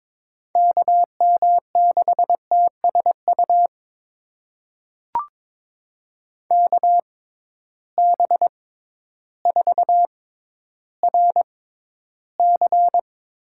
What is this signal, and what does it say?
Morse code, heard as a keyed tone, about 22 wpm, spaced as for 12 wpm, 700 Hz. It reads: KM6TSU KB4RC
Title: USA Call Signs - 6 Characters - 6x Character Spacing - 22wpm